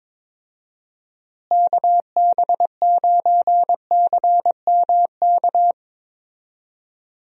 Morse code, heard as a keyed tone, about 22 wpm, 700 Hz. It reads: KB9CMK
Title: USA Call Signs - 6 Characters - Repeated 2x - 22wpm